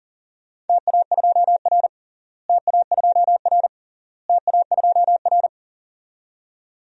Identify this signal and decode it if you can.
Morse code, heard as a keyed tone, about 40 wpm, 700 Hz. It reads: TA2R TA2R TA2R
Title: DXCC-100 Call Signs - 4 Characters - Repeated 3x - 40wpm